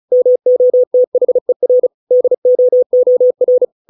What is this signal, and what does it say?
Morse code, heard as a keyed tone, about 37 wpm, 500 Hz. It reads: MOTHER DOOR